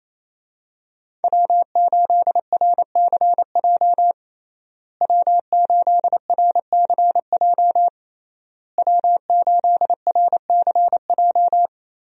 Morse code, heard as a keyed tone, about 27 wpm, 700 Hz. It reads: W8RCJ W8RCJ W8RCJ
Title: USA Call Signs - 5 Characters - Repeated 3x - 28wpm